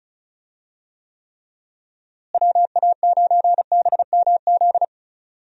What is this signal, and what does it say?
Morse code, heard as a keyed tone, about 34 wpm, 700 Hz. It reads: WA9BMZ
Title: USA Call Signs - 6 Characters - Repeated 2x - 35wpm